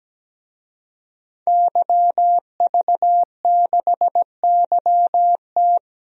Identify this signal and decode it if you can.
Morse code, heard as a keyed tone, about 17 wpm, 700 Hz. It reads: YV6YT
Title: DXCC-100 Call Signs - 5 Characters - 17wpm